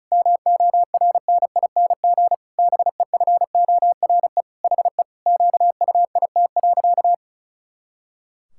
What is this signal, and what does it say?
Morse code, heard as a keyed tone, about 36 wpm, 700 Hz. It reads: MORNING BEFORE HE QUIT.
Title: Pangram Challenge v2 - 35wpm